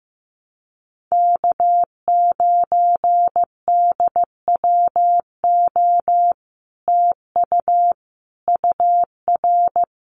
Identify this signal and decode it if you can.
Morse code, heard as a keyed tone, about 16 wpm, 700 Hz. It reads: K9DWO TU UR
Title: Common QSO Phrases Encore 15wpm